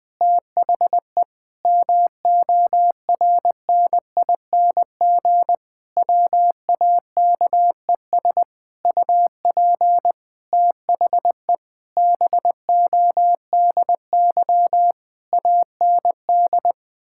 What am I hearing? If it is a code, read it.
Morse code, heard as a keyed tone, about 19 wpm, 700 Hz. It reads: THE MORNING WAKES UP THE BODY AND
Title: Difficult Sentences with Hints - 20wpm